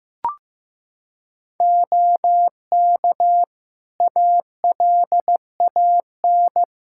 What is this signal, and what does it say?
Morse code, heard as a keyed tone, about 15 wpm, 700 Hz. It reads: OK ALAN
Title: Practice QSO - Exchange 3 - More Info - 15wpm